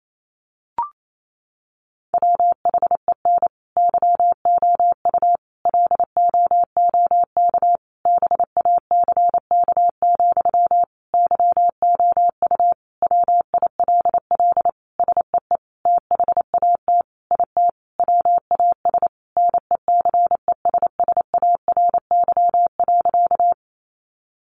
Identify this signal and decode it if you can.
Morse code, heard as a keyed tone, about 28 wpm, 700 Hz. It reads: WHEN YOU LOOK BACK, YOU WILL SEE THAT IT WAS NECESSARY.
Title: Sentences from Top 800 Words 28wpm